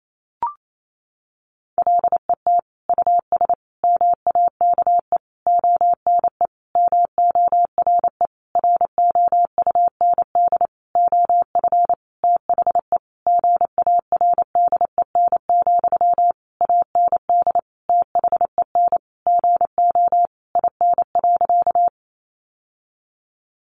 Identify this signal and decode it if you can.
Morse code, heard as a keyed tone, about 28 wpm, 700 Hz. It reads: LET US MAKE ONE MORE ROUND OF THE GARDEN, AND THEN GO IN.
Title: Sentences from Top 700 Words 28wpm